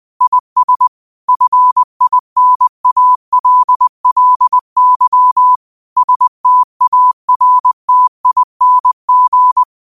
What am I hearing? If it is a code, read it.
Morse code, heard as a keyed tone, about 20 wpm, 1000 Hz. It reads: IS FINALLY STARTING
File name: qotdfiles / 20wpm